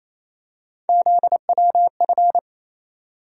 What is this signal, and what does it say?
Morse code, heard as a keyed tone, about 28 wpm, 700 Hz. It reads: ZWF
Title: USA Call Sign Postfixes - 3 Letters 28wpm